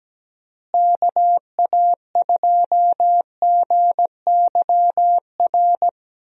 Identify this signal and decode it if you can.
Morse code, heard as a keyed tone, about 17 wpm, 700 Hz. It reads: KA2GYR
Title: USA Call Signs - 6 Characters - Repeated 3x - 17wpm